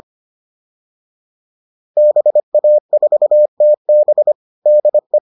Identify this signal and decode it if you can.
Morse code, heard as a keyed tone, about 25 wpm, 600 Hz. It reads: BA4TB DE